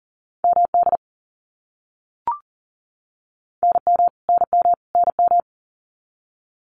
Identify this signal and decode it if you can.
Morse code, heard as a keyed tone, about 40 wpm, 700 Hz. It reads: MD NM NM NM
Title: US States Repeated 3x 40wpm